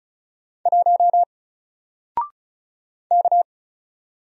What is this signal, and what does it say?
Morse code, heard as a keyed tone, about 36 wpm, 700 Hz. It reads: 1 K
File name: Single-Letter-Number-35wpm